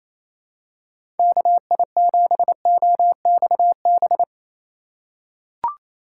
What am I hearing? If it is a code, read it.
Morse code, heard as a keyed tone, about 27 wpm, 700 Hz. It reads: KI7OXB
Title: USA Call Signs - 6 Characters - 4x Character Spacing - 28wpm